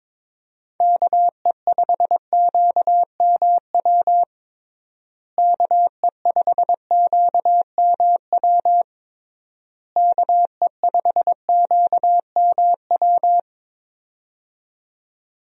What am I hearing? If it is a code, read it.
Morse code, heard as a keyed tone, about 22 wpm, 700 Hz. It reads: KE5QMW KE5QMW KE5QMW
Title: USA Call Signs - 6 Characters - Repeated 3x - 22wpm